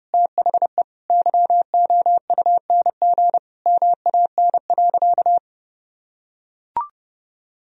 Morse code, heard as a keyed tone, about 30 wpm, 700 Hz. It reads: THE YOUNG MAN.
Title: Sentences from Top 400 Words and Abbreviations 30wpm